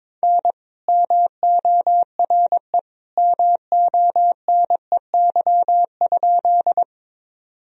Morse code, heard as a keyed tone, about 23 wpm, 700 Hz. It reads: N MORE MONEY?
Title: Sentences from Top 400 Words and Abbreviations 22wpm